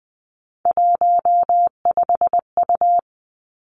Morse code, heard as a keyed tone, about 20 wpm, 700 Hz. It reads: 15U
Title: Three Challenging Letter-Number Combinations 20wpm